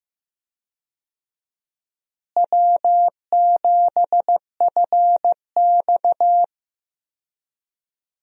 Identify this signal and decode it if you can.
Morse code, heard as a keyed tone, about 15 wpm, 700 Hz. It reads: W7FX
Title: USA Call Signs - 4 Characters - Repeated 3x - 15wpm